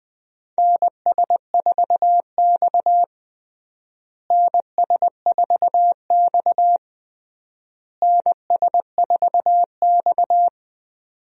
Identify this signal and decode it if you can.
Morse code, heard as a keyed tone, about 20 wpm, 700 Hz. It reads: NS4X NS4X NS4X
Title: USA Call Signs - 4 Characters - Repeated 3x - 20wpm